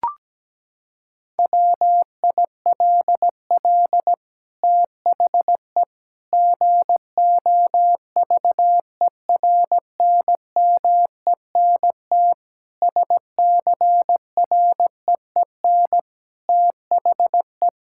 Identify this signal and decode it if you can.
Morse code, heard as a keyed tone, about 17 wpm, 700 Hz. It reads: WILL THE GOVERNMENT SCREEN THE
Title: Sentences from Top 1000 Words - Review Part 6 - 17wpm